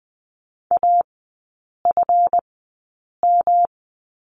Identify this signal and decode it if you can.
Morse code, heard as a keyed tone, about 20 wpm, spaced as for 8 wpm, 700 Hz. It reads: AFM
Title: 178 Lesson 29e - Slash - Three Characters 20wpm